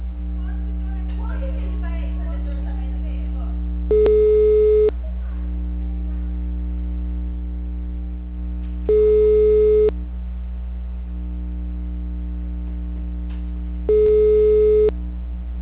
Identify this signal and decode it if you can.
Ringback tone (Europe, 425 Hz), with speech.